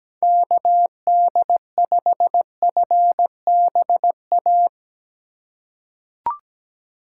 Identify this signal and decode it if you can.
Morse code, heard as a keyed tone, about 17 wpm, 700 Hz. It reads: KD5FBA
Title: K1USN SST QSO Practice - 17wpm